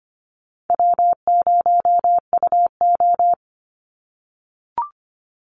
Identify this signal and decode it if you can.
Morse code, heard as a keyed tone, about 25 wpm, 700 Hz. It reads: W0UO
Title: Call Signs - 25wpm